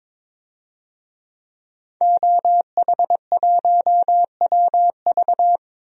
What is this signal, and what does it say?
Morse code, heard as a keyed tone, about 22 wpm, 700 Hz. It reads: OH1WV
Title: DXCC-100 Call Signs - 5 Characters - 22wpm